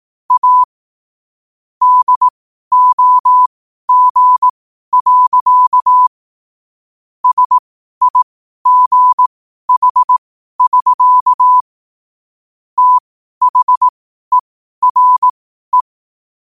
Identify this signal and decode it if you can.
Morse code, heard as a keyed tone, about 18 wpm, spaced as for 12 wpm, 1000 Hz. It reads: A DOG. SIGH<SK> THERE
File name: qotdfiles / 12wpm